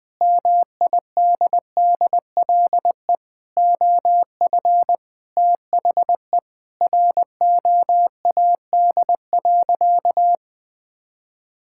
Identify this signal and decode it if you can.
Morse code, heard as a keyed tone, about 20 wpm, 700 Hz. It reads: MIDDLE OF THE ROAD.
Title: Sentences with 3-Word Phrases - 20wpm